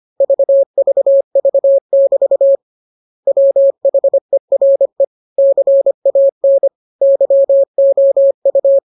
Morse code, heard as a keyed tone, about 25 wpm, 550 Hz. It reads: VVV<BT> WHERE CAN YOU
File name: jokes / joke 10 25 0000